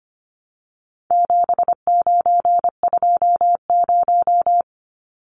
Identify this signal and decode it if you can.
Morse code, heard as a keyed tone, about 25 wpm, 700 Hz. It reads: 7920
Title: Numbers - 4-Digit 25wpm